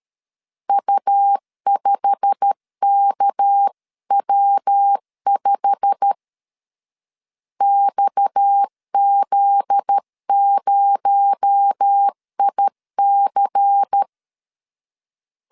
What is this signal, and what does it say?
Morse code, heard as a keyed tone, about 13 wpm, 800 Hz. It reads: U5KW5 XZ0IC
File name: LEZIONI / 8 YJ45 / 8 YJ45 12wpm 2151